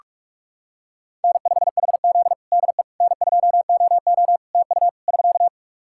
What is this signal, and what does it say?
Morse code, heard as a keyed tone, about 43 wpm, 700 Hz. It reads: N5HZ DE N2OO TU <SK>
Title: Practice QSO - Exchange 4 - Ending - 45wpm